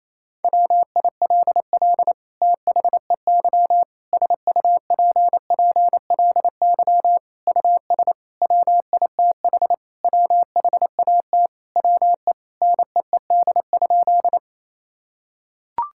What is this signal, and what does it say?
Morse code, heard as a keyed tone, about 27 wpm, 700 Hz. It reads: WILL THEY SUPPLY US WITH WHAT WE NEED?
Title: Sentences from Top 1000 Words - Review Part 5 - 28wpm